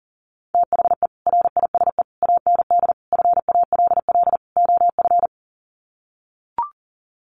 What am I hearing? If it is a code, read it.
Morse code, heard as a keyed tone, about 40 wpm, 700 Hz. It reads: THE RISE AND FALL OF
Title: Sets of 5 Words Encore2 40wpm